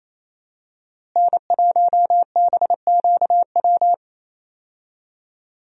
Morse code, heard as a keyed tone, about 27 wpm, 700 Hz. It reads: N1BQW
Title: USA Call Signs - 5 Characters - Repeated 2x - 28wpm